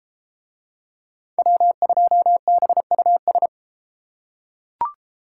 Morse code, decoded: W2BUS

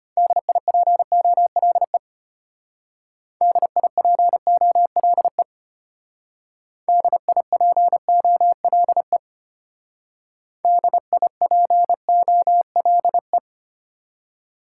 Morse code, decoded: DIPOLE DIPOLE DIPOLE DIPOLE

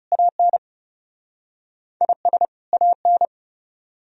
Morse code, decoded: AN IS AN

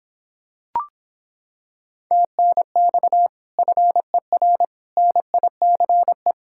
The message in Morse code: TNX FER NICE

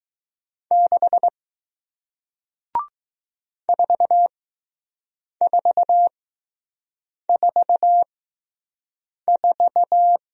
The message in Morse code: 6 4444

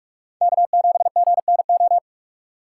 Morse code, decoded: K7GNO

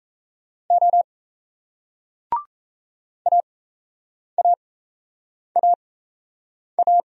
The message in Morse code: O AAAA

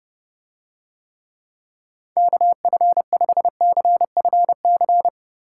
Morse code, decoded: KF5CFC

